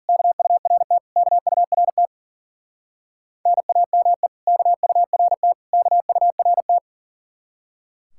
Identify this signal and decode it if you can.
Morse code, heard as a keyed tone, about 48 wpm, 700 Hz. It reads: KURT KURT NAME KURT KURT
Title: Speed-Racing Common QSO Phrases - 40wpm